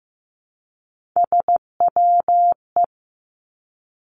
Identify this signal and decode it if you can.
Morse code, heard as a keyed tone, about 15 wpm, 700 Hz. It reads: SWE
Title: Three Challenging Letter-Number Combinations 15wpm